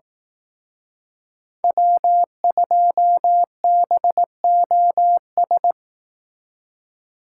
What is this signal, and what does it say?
Morse code, heard as a keyed tone, about 18 wpm, 700 Hz. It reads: W2BOS
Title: Speed Racing - USA 1x3 Call Signs 15wpm